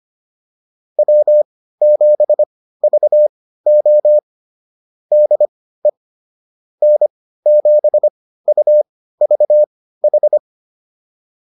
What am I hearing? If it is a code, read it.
Morse code, heard as a keyed tone, about 25 wpm, spaced as for 15 wpm, 600 Hz. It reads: W7VO DE N7UVH